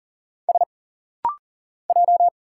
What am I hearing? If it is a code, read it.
Morse code, heard as a keyed tone, about 40 wpm, 700 Hz. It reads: S J
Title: Single Letter-Number Alt - Rapid-Fire 40wpm